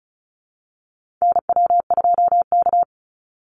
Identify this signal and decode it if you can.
Morse code, heard as a keyed tone, about 34 wpm, 700 Hz. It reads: NW2K
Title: Call Signs Repeated 2x - 35wpm